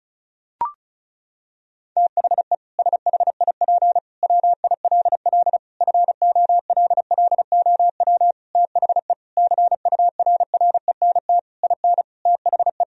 Morse code, heard as a keyed tone, about 36 wpm, 700 Hz. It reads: THE SHIP WILL FOLLOW THE CURRENT IN THE
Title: Sentences from Top 1000 Words - Review Part 4 - 35wpm